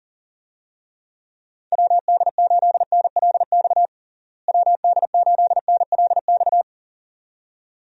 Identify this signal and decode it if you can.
Morse code, heard as a keyed tone, about 40 wpm, 700 Hz. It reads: WD8NLX WD8NLX
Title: USA Call Signs - 6 Characters - Repeated 2x - 40wpm